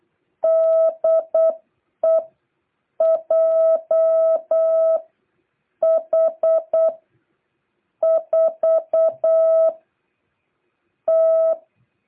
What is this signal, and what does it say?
Morse code, heard as a keyed tone, about 8 wpm, spaced as for 5 wpm, 660 Hz. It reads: DE JH4T